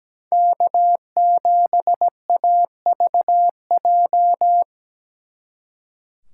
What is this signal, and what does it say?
Morse code, heard as a keyed tone, about 20 wpm, 700 Hz. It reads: K7AVJ
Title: Speed Racing - USA 1x3 Call Signs 17wpm